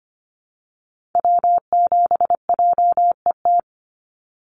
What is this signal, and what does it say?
Morse code, heard as a keyed tone, about 25 wpm, 700 Hz. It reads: W7JET
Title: NA SOTA Activator Callsigns 25wpm